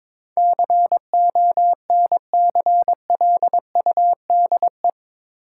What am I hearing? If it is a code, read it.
Morse code, heard as a keyed tone, about 21 wpm, 700 Hz. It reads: CONCLUDE
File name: practice / 22wpm / Eight Letter Words - 22wpm